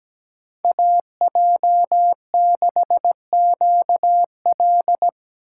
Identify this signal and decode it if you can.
Morse code, heard as a keyed tone, about 17 wpm, 700 Hz. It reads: AJ6QL
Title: K1USN SST QSO Practice - 17wpm